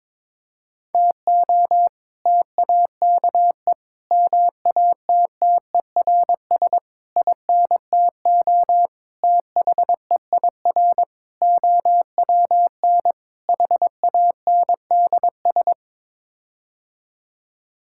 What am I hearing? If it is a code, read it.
Morse code, heard as a keyed tone, about 22 wpm, 700 Hz. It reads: TO TAKE MATTERS INTO THEIR OWN HANDS
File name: practice / 22wpm / Sets of 7 Words - 22wpm